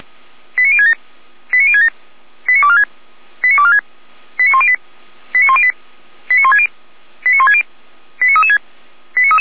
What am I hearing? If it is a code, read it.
Five-tone selective call (ZVEI), heard as quick runs of five tones, tones of about 70 ms. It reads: 89067 89067 89246 89246 89108 89108 89160 89160 89336
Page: /SDR/ZVEI/